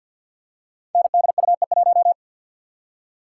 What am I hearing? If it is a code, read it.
Morse code, heard as a keyed tone, about 52 wpm, 700 Hz. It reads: NDUE1
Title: Code Groups - 5 Letter:Numbers - 50wpm